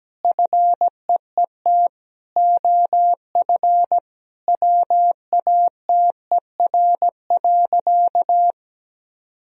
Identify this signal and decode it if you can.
Morse code, heard as a keyed tone, about 17 wpm, 700 Hz. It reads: FEET OF WATER.
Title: Sentences from Top 1000 Words - Review Part 1 - 17wpm